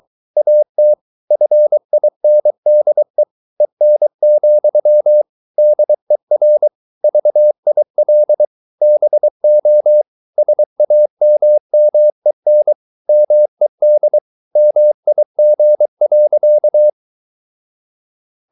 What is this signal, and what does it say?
Morse code, heard as a keyed tone, about 23 wpm, 600 Hz. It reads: AT FINDE EN, DER VIL BO SAMMEN MED MIG.
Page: Ferie 23 wpm